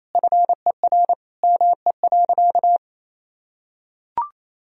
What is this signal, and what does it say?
Morse code, heard as a keyed tone, about 28 wpm, 700 Hz. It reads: FER ME.
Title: Sentences from Top 300 Words and Abbreviations 28wpm